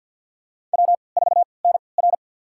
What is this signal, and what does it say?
Morse code, heard as a keyed tone, about 53 wpm, spaced as for 29 wpm, 700 Hz. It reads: W4NR
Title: USA Call Signs - 4 Characters - 3x Character Spacing - 50wpm